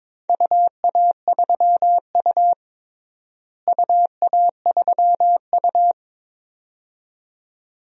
Morse code, heard as a keyed tone, about 21 wpm, 700 Hz. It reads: UA3U UA3U
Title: DXCC-100 Call Signs - 4 Characters - Repeated 2x - 22wpm